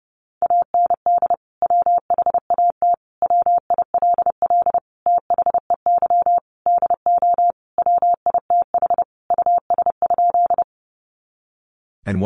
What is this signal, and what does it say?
Morse code, heard as a keyed tone, about 30 wpm, 700 Hz. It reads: AND WHAT WILL THEY DO WITH US?
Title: Sentences from Top 200 Words 30wpm